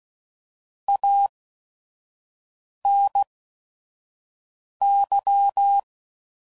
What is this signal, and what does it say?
Morse code, heard as a keyed tone, about 16 wpm, spaced as for 5 wpm, 800 Hz. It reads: ANY
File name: CW Code Groups @5WPM / Technical Subjects / OR Gate